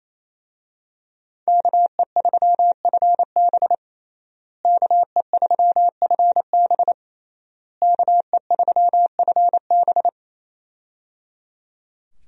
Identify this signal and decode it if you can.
Morse code, heard as a keyed tone, about 28 wpm, 700 Hz. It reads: KE3FB KE3FB KE3FB
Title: USA Call Signs - 5 Characters - Repeated 3x - 28wpm